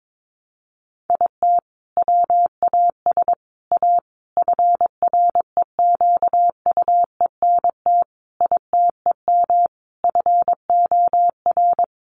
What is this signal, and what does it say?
Morse code, heard as a keyed tone, about 22 wpm, 700 Hz. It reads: IT WAS A FREQUENT ITEM FOR
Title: Sentences from Top 1400 Words 22wpm